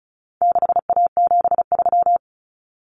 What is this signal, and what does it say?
Morse code, heard as a keyed tone, about 34 wpm, 700 Hz. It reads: 6A73
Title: Common Cut Numbers - 4-Digit 35wpm